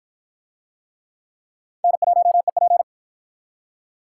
Morse code, heard as a keyed tone, about 46 wpm, 700 Hz. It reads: N1EP